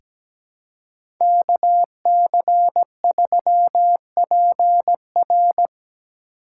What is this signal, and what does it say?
Morse code, heard as a keyed tone, about 17 wpm, 700 Hz. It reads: KC3PR
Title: USA Call Signs - 5 Characters - 4x Character Spacing - 17wpm